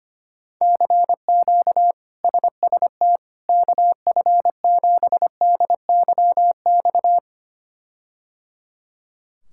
Morse code, heard as a keyed tone, about 25 wpm, 700 Hz. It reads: CQ SST KF7DYX
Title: K1USN SST QSO Practice - 25wpm